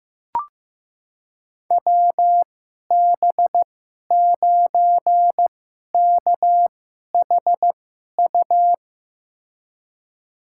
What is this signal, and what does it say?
Morse code, heard as a keyed tone, about 16 wpm, spaced as for 11 wpm, 700 Hz. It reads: WB9KHU
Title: USA Call Signs - 2x Character Spacing - 15wpm